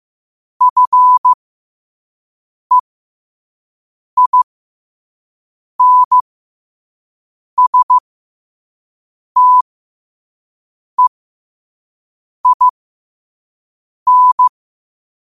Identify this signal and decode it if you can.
Morse code, heard as a keyed tone, about 15 wpm, spaced as for 5 wpm, 1000 Hz. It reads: FEINSTEIN
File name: qotdfiles / 5wpm